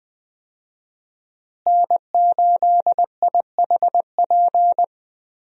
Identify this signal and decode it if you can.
Morse code, heard as a keyed tone, about 20 wpm, 700 Hz. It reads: N8IHP
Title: USA Call Signs - 5 Characters - Repeated 3x - 20wpm